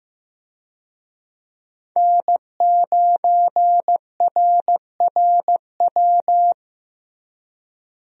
Morse code, heard as a keyed tone, about 15 wpm, 700 Hz. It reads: N9RRW
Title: USA Call Signs - 5 Characters - Repeated 2x - 15wpm